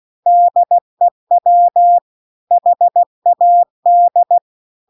Morse code, heard as a keyed tone, about 16 wpm, 700 Hz. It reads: DEW HAD